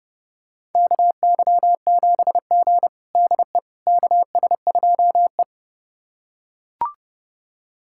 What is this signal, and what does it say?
Morse code, heard as a keyed tone, about 29 wpm, 700 Hz. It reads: KY7G DE KS2E
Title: USA Callsign DE Callsign - 4 Characters - 30wpm